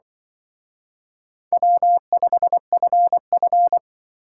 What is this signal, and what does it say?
Morse code, heard as a keyed tone, about 24 wpm, 700 Hz. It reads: W5FF
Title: Speed Racing - USA 1x2 Call Signs 20wpm